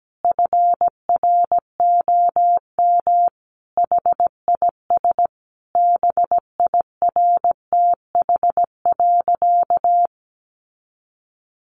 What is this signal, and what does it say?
Morse code, heard as a keyed tone, about 17 wpm, 700 Hz. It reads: FROM HIS BIRTH.
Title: Sentences from Top 900 Words 17wpm